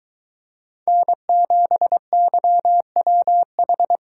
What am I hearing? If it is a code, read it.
Morse code, heard as a keyed tone, about 23 wpm, 700 Hz. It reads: N7YWH